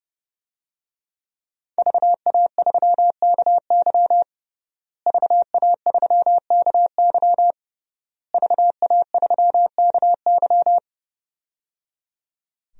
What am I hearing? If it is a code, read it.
Morse code, heard as a keyed tone, about 30 wpm, 700 Hz. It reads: VA3KY VA3KY VA3KY
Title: DXCC-100 Call Signs - 5 Characters - Repeated 3x - 30wpm